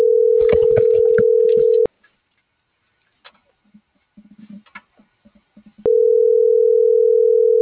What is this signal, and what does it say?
Ringback tone (North America), 440 + 480 Hz.